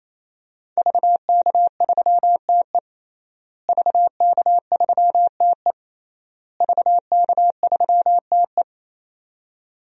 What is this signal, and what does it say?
Morse code, heard as a keyed tone, about 28 wpm, 700 Hz. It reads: VK3TE VK3TE VK3TE
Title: DXCC-100 Call Signs - 5 Characters - Repeated 3x - 28wpm